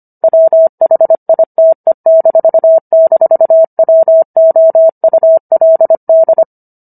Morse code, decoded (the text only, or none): WHITE--WOULD